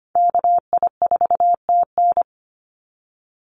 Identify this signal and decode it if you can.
Morse code, heard as a keyed tone, about 25 wpm, 690 Hz. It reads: KI4TN
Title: NA SOTA Activator Callsigns 25wpm